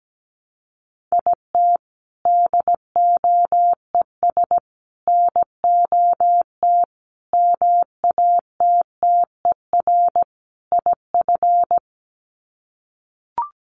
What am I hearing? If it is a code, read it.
Morse code, heard as a keyed tone, about 17 wpm, 700 Hz. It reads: IT DOES NOT MATTER IF
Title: Sets of 5 Words Encore 17wpm